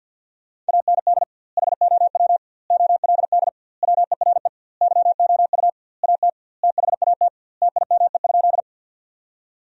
Morse code, heard as a keyed tone, about 50 wpm, 700 Hz. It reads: AND HOW OLD WERE YOU AT THAT TIME?
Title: 2x Word-Spacing - Sentences from Top 200 Words - 50wpm